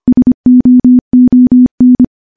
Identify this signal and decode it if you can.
Morse code, heard as a keyed tone, about 25 wpm, 250 Hz. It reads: SOON
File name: morse-training-25WPM